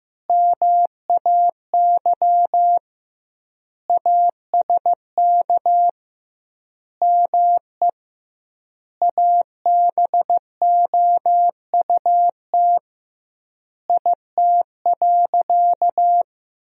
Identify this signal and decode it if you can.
Morse code, heard as a keyed tone, about 15 wpm, 700 Hz. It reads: MAY ASK ME ABOUT IT.
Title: 2x Word-Spacing - Sentences from Top 200 Words - 15wpm